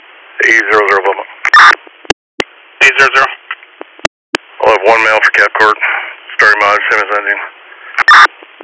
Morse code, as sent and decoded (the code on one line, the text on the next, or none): none